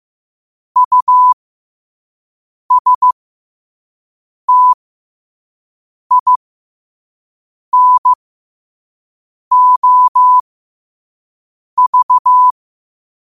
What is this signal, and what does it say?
Morse code, heard as a keyed tone, about 15 wpm, spaced as for 5 wpm, 1000 Hz. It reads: USTINOV
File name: qotdfiles / 5wpm